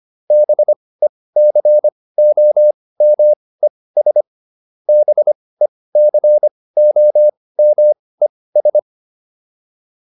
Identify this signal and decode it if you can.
Morse code, heard as a keyed tone, about 25 wpm, spaced as for 18 wpm, 600 Hz. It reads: BECOMES BECOMES